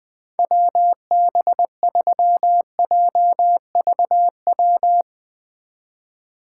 Morse code, heard as a keyed tone, about 20 wpm, 700 Hz. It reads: WB3JVW
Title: Speed Racing - USA Call Signs 17wpm